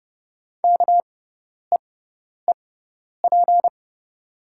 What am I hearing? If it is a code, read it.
Morse code, heard as a keyed tone, about 30 wpm, spaced as for 10 wpm, 700 Hz. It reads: KEEP
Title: 6x Character Spacing - Top 300 Words - 30wpm